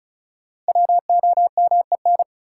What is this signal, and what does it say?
Morse code, heard as a keyed tone, about 36 wpm, 700 Hz. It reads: WOMEN
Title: Sets of 3 Words - New Word Drill - 35wpm